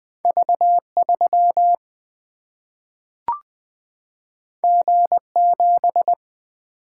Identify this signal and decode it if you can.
Morse code, heard as a keyed tone, about 20 wpm, 700 Hz. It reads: V 3 G 7
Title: Single-Letter Single-Number 20wpm